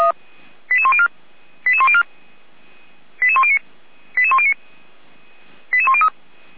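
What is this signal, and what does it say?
Five-tone selective call (ZVEI), heard as quick runs of five tones, tones of about 70 ms. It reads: 88104 88104 88109 88109 88103
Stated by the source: ZVEI five-tone call received by SDR, from feuerwehr-orlamuende (fire brigade Orlamünde).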